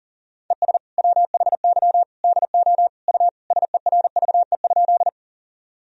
Morse code, heard as a keyed tone, about 37 wpm, 700 Hz. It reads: ES WHY DO U SERVE?
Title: Sentences from Top 400 Words and Abbreviations 40wpm